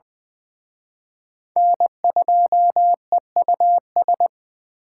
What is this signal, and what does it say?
Morse code, heard as a keyed tone, about 23 wpm, 700 Hz. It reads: N2EUS
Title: Speed Racing - USA 1x3 Call Signs 20wpm